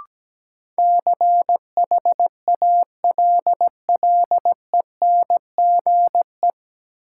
Morse code, heard as a keyed tone, about 17 wpm, 700 Hz. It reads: CHALLENGE
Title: Sentences from Top 1000 Words - Review Part 7 - New Word Drill - 17wpm